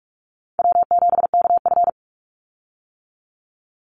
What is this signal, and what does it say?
Morse code, heard as a keyed tone, about 46 wpm, 700 Hz. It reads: W7KF
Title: Call Signs - 45wpm